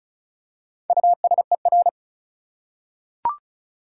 Morse code, heard as a keyed tone, about 36 wpm, 700 Hz. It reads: USER
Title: Four Letter Words - 35wpm